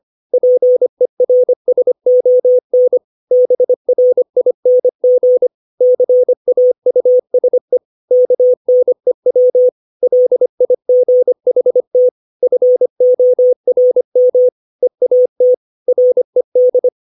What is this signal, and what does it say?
Morse code, heard as a keyed tone, about 25 wpm, 500 Hz. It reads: PERSON BRING CAUSE KNEW LIGHT FORM EAT RED
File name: 632 words / 632 Words 25WPM